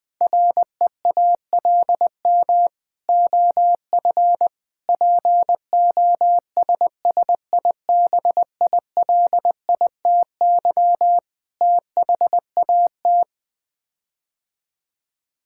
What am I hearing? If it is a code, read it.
Morse code, heard as a keyed tone, about 20 wpm, 700 Hz. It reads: REALM OF POSSIBILITY THAT
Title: Sets of 9 Words - 20wpm